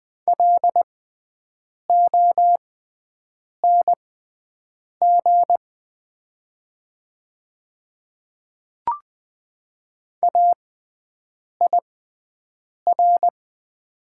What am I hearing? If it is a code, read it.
Morse code, heard as a keyed tone, about 20 wpm, spaced as for 7 wpm, 700 Hz. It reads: LONG AIR
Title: 6x Character Spacing - Top 200 Words - 20wpm